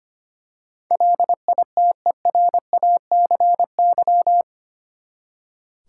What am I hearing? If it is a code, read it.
Morse code, heard as a keyed tone, about 25 wpm, 700 Hz. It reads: LITERACY
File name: Eight Letter Words - 25wpm